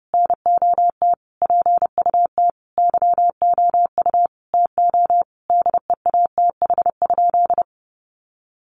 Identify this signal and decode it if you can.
Morse code, heard as a keyed tone, about 30 wpm, 700 Hz. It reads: NOT PUT YOU TO DEATH?
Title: Sentences from Top 1100 Words 30wpm